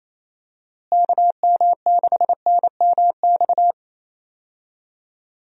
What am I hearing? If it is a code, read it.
Morse code, heard as a keyed tone, about 28 wpm, 700 Hz. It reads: KM6NMX
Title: USA Call Signs - 6 Characters - Repeated 2x - 28wpm